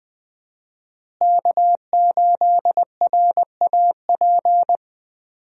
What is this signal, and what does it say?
Morse code, heard as a keyed tone, about 20 wpm, 700 Hz. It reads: K8RAP
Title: USA Call Signs - 5 Characters - Repeated 3x - 20wpm